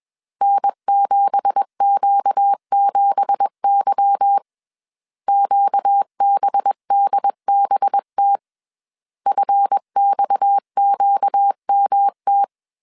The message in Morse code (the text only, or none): N7Q7Y QBDBT FXQMT